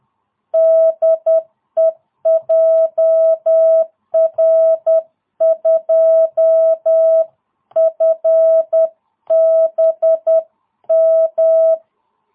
Morse code, heard as a keyed tone, about 10 wpm, 660 Hz. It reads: DE JR2FBM